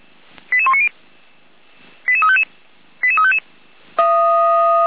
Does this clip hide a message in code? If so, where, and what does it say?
Yes: five-tone selective call (ZVEI) from 0.5 to 3.4 s, heard as quick runs of five tones, tones of about 70 ms. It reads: 88190 88355 88355